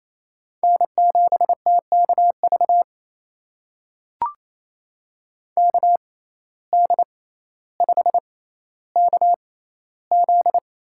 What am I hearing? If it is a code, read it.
Morse code, heard as a keyed tone, about 28 wpm, spaced as for 14 wpm, 700 Hz. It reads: N7TKV KD5KZ
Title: USA Call Signs - 5 Characters - 6x Character Spacing - 28wpm